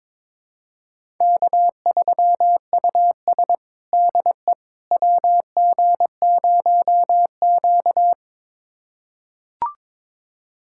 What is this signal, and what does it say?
Morse code, heard as a keyed tone, about 22 wpm, 700 Hz. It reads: K3US DE WG0Q
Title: USA Callsign DE Callsign - 4 Characters - 22wpm